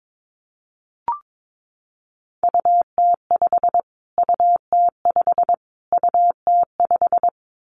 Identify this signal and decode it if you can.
Morse code, heard as a keyed tone, about 22 wpm, 700 Hz. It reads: UT5 UT5 UT5
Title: Callsign Prefixes for Top 25 DXCC - Repeated 3x - 22wpm